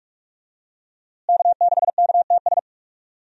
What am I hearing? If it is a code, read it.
Morse code, heard as a keyed tone, about 45 wpm, 700 Hz. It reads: K6KTS
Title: USA Call Signs - 5 Characters - Repeated 2x - 45wpm